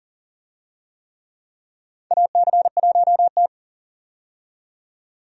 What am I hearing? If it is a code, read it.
Morse code, heard as a keyed tone, about 38 wpm, 700 Hz. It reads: AC1T
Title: USA Call Signs - 4 Characters - 40wpm